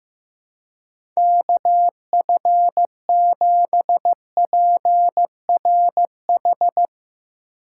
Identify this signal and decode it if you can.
Morse code, heard as a keyed tone, about 15 wpm, 700 Hz. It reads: KF7PRH
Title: USA Call Signs - 6 Characters - 4x Character Spacing - 15wpm